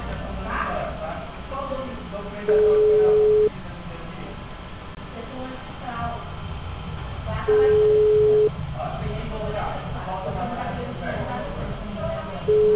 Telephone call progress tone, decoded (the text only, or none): ringback tone